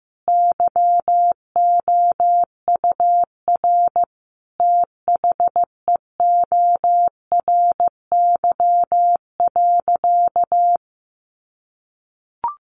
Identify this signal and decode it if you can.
Morse code, heard as a keyed tone, about 15 wpm, 700 Hz. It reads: YOUR THEORY.
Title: Sentences from Top 800 Words 15wpm